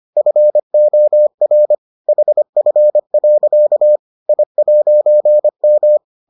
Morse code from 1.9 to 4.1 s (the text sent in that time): HF.